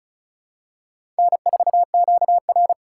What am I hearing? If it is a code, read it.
Morse code, heard as a keyed tone, about 34 wpm, 700 Hz. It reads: N4QR
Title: USA Call Signs - 4 Characters - 6x Character Spacing - 35wpm